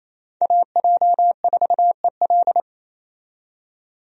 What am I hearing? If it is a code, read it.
Morse code, heard as a keyed tone, about 28 wpm, 700 Hz. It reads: AJ4EL
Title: USA Call Signs - 5 Characters - 28wpm